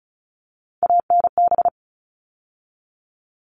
Morse code, heard as a keyed tone, about 34 wpm, 700 Hz. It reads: ANB
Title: All Cut Numbers - 3-Digit 35wpm